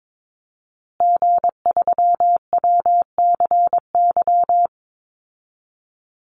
Morse code, heard as a keyed tone, about 22 wpm, 700 Hz. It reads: G3WCY
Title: Call Signs Repeated 3x - 22wpm